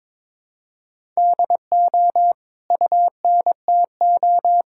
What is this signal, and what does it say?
Morse code, heard as a keyed tone, about 22 wpm, 700 Hz. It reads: DO UNTO
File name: practice / 22wpm / Sets of 8 Words - 22wpm